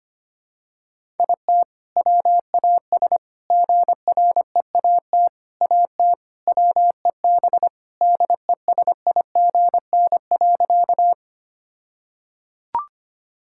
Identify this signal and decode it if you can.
Morse code, heard as a keyed tone, about 25 wpm, 700 Hz. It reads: IT WAS GREAT AT WEB DESIGN.
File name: practice / 25wpm / All-Age Appropriate Jokes - 25wpm